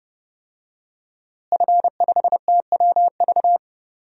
Morse code, heard as a keyed tone, about 30 wpm, 700 Hz. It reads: F5TWV
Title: DXCC-100 Call Signs - 5 Characters - Repeated 3x - 30wpm